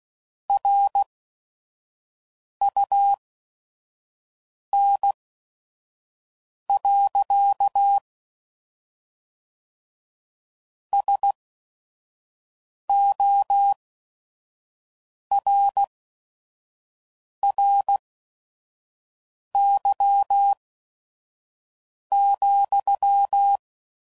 Morse code, heard as a keyed tone, about 16 wpm, spaced as for 5 wpm, 800 Hz. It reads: RUN. SORRY,
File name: CW Code Groups @5WPM / QSOs / kc7sac